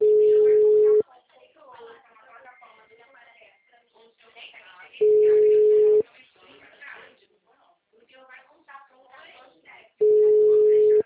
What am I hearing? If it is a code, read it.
Ringback tone (Europe, 425 Hz), with speech.
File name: out- -2014- - - .92373